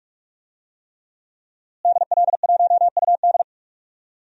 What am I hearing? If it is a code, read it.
Morse code, heard as a keyed tone, about 41 wpm, 700 Hz. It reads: DL1UD
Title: DXCC-100 Call Signs - 5 Characters - Repeated 3x - 45wpm